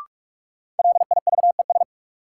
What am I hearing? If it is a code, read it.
Morse code, heard as a keyed tone, about 46 wpm, 700 Hz. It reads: LIVES
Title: Sets of 5 Words Encore - New Word Drill - 45wpm